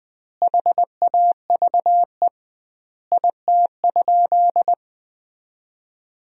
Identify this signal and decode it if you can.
Morse code, heard as a keyed tone, about 20 wpm, 700 Hz. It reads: HAVE IT?
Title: 2x Word-Spacing - Sentences from Top 100 Words - 20wpm